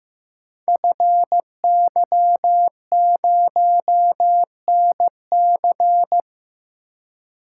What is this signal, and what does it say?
Morse code, heard as a keyed tone, about 15 wpm, 700 Hz. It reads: FY0NC
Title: Code Groups - 5 Letter:Numbers - 15wpm